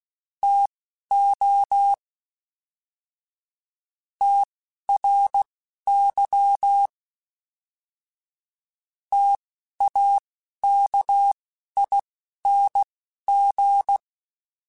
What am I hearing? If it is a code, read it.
Morse code, heard as a keyed tone, about 16 wpm, spaced as for 12 wpm, 800 Hz. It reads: TO TRY TAKING